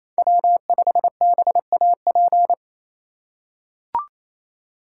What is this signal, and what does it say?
Morse code, heard as a keyed tone, about 28 wpm, 700 Hz. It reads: W5BAP
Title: USA Call Signs - 6x Character Spacing - 28wpm